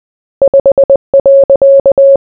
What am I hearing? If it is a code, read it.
Morse code, heard as a keyed tone, about 20 wpm, 550 Hz. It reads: 5.